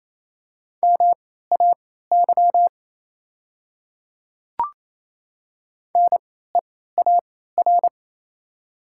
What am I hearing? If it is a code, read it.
Morse code, heard as a keyed tone, about 28 wpm, spaced as for 16 wpm, 690 Hz. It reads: MAY NEAR
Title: 3x Character Spacing - Top 200 Words - 28wpm